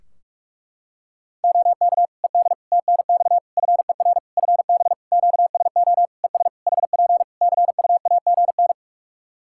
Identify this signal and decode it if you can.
Morse code, heard as a keyed tone, about 46 wpm, 700 Hz. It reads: OK ED TNX FER FB QSO ES HP CUAGN
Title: K1USN SST QSO Practice 45wpm